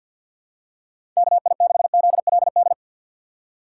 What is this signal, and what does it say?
Morse code, heard as a keyed tone, about 51 wpm, 700 Hz. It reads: KI6ZLD
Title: USA Call Signs - 6 Characters - 4x Character Spacing - 50wpm